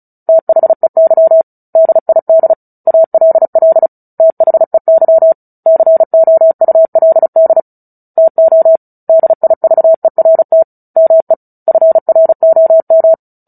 Morse code, heard as a keyed tone, about 34 wpm, 650 Hz. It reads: THEY DID ALL THEY COULD TO DIVERT ME FROM
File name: Book Two / 09-Chapter 9 - 35 WPM